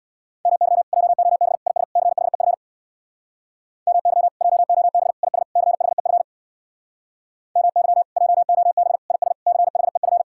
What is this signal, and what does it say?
Morse code, heard as a keyed tone, about 65 wpm, 700 Hz. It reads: MY JOB IS CHEF MY JOB IS CHEF MY JOB IS CHEF